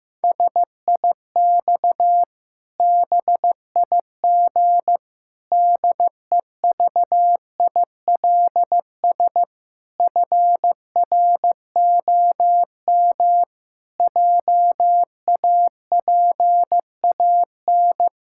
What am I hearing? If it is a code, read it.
Morse code, heard as a keyed tone, about 15 wpm, 700 Hz. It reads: SIX BIG DEVILS FROM JAPAN
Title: Pangram Challenge v2 - 15wpm